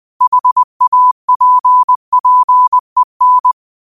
Morse code, heard as a keyed tone, about 20 wpm, 1000 Hz. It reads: HAPPEN
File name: qotdfiles / 20wpm